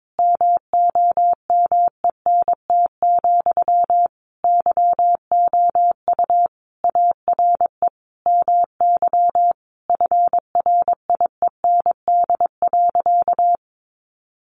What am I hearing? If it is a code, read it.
Morse code, heard as a keyed tone, about 22 wpm, 700 Hz. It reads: MOMENT, YOU ARE MY FRIEND.
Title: Sentences from Top 1000 Words 22wpm